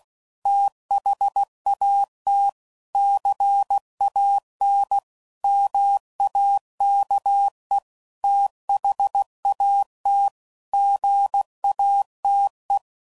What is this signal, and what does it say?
Morse code, heard as a keyed tone, about 16 wpm, 800 Hz. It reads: THAT CAN MAKE THAT GATE
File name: CW Code Groups @15WPM / Technical Subjects / OR Gate